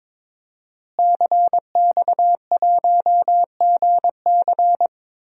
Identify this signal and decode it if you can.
Morse code, heard as a keyed tone, about 22 wpm, 700 Hz. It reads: CX1GC
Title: DXCC-100 Call Signs - 5 Characters - Repeated 3x - 22wpm